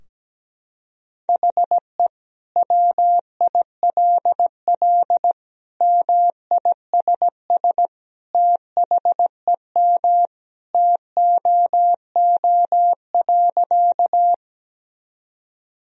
Morse code, heard as a keyed tone, about 17 wpm, 700 Hz. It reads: HE WILL MISS THEM TOO.
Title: Sentences from Top 1000 Words - Review Part 3 - 17wpm